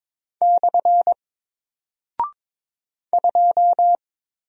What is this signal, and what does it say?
Morse code, heard as a keyed tone, about 22 wpm, 700 Hz. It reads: / 2
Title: Single Character (Letter:Number:Punctuation:Prosign) 22wpm